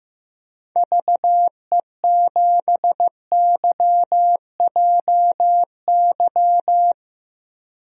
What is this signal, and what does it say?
Morse code, heard as a keyed tone, about 15 wpm, 700 Hz. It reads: VE7YJY
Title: DXCC-100 Call Signs - 6 Characters - Repeated 2x - 15wpm